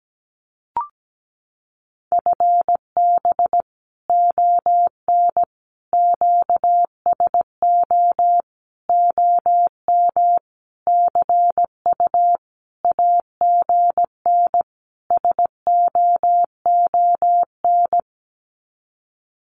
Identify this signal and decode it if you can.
Morse code, heard as a keyed tone, about 17 wpm, 700 Hz. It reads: FB ON QSO OM CU AGN SOON
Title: Common QSO Phrases 17wpm